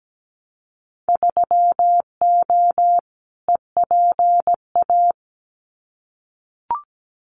Morse code, heard as a keyed tone, about 17 wpm, 700 Hz. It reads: 3O EPA
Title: Winter Field Day Exchanges 17wpm